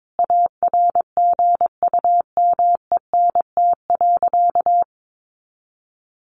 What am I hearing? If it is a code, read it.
Morse code, heard as a keyed tone, about 22 wpm, 700 Hz. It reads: ARGUMENT.
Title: Sentences from Top 1400 Words 22wpm